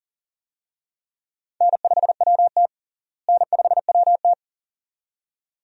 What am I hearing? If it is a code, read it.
Morse code, heard as a keyed tone, about 40 wpm, 700 Hz. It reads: N5WT N5WT
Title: USA Call Signs - 4 Characters - Repeated 2x - 40wpm